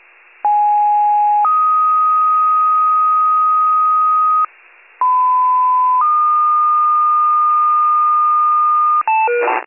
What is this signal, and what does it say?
Two-tone page, heard as a one-second tone, then a three-second tone, about 830 then 1250 Hz, about 1010 then 1220 Hz. It is heard from 0.4 to 9.0 s.
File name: pager / 94 RESCUE / wcalert